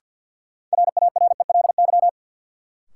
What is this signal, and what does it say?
Morse code, heard as a keyed tone, about 52 wpm, 700 Hz. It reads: RARELY
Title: Sentences from Top 1400 Words - New Word Drill - 50wpm